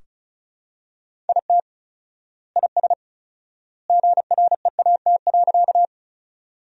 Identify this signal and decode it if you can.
Morse code, heard as a keyed tone, about 35 wpm, 700 Hz. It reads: IT IS GREAT.
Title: 4x Word-Spacing - Sentences from Top 200 Words - 35wpm